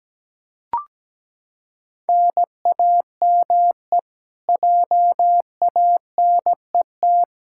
Morse code, heard as a keyed tone, about 17 wpm, 700 Hz. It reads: NAME JANET
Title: Practice QSO - Exchange 1 - Beginning - 17wpm